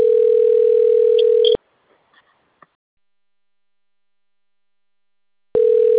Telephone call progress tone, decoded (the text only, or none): ringback tone